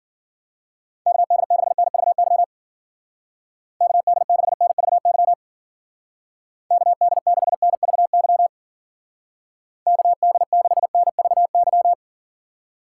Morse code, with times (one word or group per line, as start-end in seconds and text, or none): 1.1-2.4 KD6NVY
3.8-5.3 KD6NVY
6.7-8.5 KD6NVY
9.9-11.9 KD6NVY